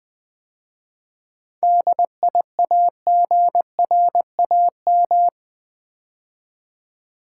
Morse code, heard as a keyed tone, about 20 wpm, 700 Hz. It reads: DIAGRAM
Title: 7 Letter Words - First 3 Letters Given - 20wpm